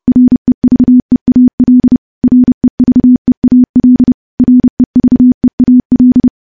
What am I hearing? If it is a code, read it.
Morse code, heard as a keyed tone, about 32 wpm, 250 Hz. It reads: REVEAL REVEAL REVEAL
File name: morse / morse-training-30WPM